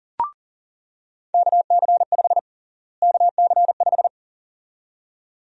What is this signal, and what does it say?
Morse code, heard as a keyed tone, about 40 wpm, 700 Hz. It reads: KC5 KC5
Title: USA Call Sign Prefixes - Repeated 2x - 40wpm